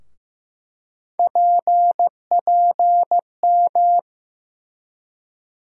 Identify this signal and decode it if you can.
Morse code, heard as a keyed tone, about 15 wpm, 700 Hz. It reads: PPM
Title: USA Call Sign Postfixes - 3 Letters 15wpm